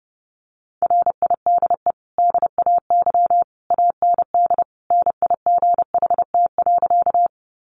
Morse code, heard as a keyed tone, about 28 wpm, 700 Hz. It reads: RIDE DAY AND NIGHT.
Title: Sentences from Top 1200 Words 30wpm